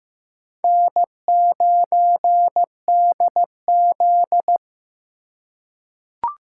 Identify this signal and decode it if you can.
Morse code, heard as a keyed tone, about 15 wpm, 700 Hz. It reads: N9DZ
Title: USA Call Signs - 4 Characters - 4x Character Spacing - 15wpm